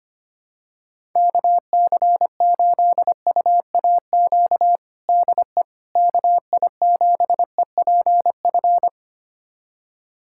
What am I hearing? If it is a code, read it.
Morse code, heard as a keyed tone, about 25 wpm, 700 Hz. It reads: KC8UAQ DE KI7EPF
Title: USA Callsign DE Callsign - 6 Characters - 25wpm